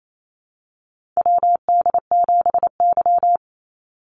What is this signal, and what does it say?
Morse code, heard as a keyed tone, about 28 wpm, 700 Hz. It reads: WD7Y
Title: NA SOTA Chaser Callsigns 28wpm